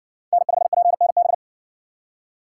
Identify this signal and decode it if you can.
Morse code, heard as a keyed tone, about 62 wpm, 700 Hz. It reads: N5PNB